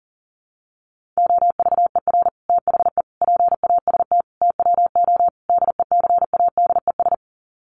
Morse code, heard as a keyed tone, about 40 wpm, 700 Hz. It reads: OVER THE PAST TWO DECADES
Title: Sets of 5 Words Encore2 40wpm